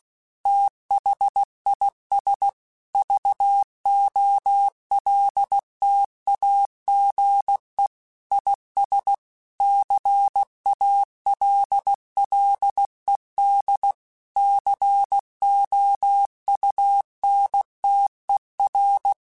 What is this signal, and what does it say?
Morse code, heard as a keyed tone, about 16 wpm, 800 Hz. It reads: THIS VOLTAGE IS CALLED COUNTER
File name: Technical Subjects / Technical Subjects-mp3s / XL @15